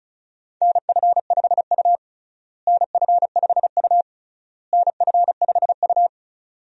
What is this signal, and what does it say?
Morse code, heard as a keyed tone, about 34 wpm, 700 Hz. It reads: NF5U NF5U NF5U
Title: USA Call Signs - 4 Characters - Repeated 3x - 35wpm